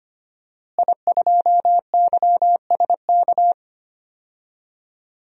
Morse code, heard as a keyed tone, about 25 wpm, 700 Hz. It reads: I2YSK
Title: DXCC-100 Call Signs - 5 Characters - Repeated 2x - 25wpm